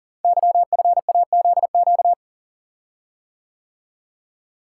Morse code, heard as a keyed tone, about 38 wpm, 700 Hz. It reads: YFAZQ
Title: Code Groups - 5 Letter:Numbers - 40wpm